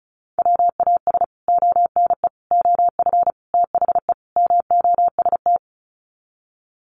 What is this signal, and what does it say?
Morse code, heard as a keyed tone, about 34 wpm, 700 Hz. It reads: WAS ONE OF THE MOST
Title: Sets of 5 Words 35wpm